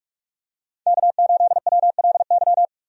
Morse code, heard as a keyed tone, about 43 wpm, 700 Hz. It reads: K8WLY
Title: USA Call Signs - 5 Characters - 45wpm